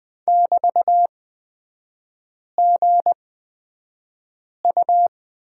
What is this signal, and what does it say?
Morse code, heard as a keyed tone, about 20 wpm, spaced as for 5 wpm, 700 Hz. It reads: <BT>GU
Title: 243 Lesson 39e - Letter z - Three Characters 20wpm